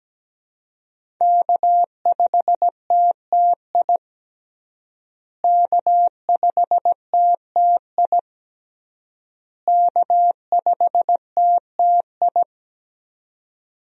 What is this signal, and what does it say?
Morse code, heard as a keyed tone, about 17 wpm, 700 Hz. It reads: K5TTI K5TTI K5TTI
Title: USA Call Signs - 5 Characters - Repeated 3x - 17wpm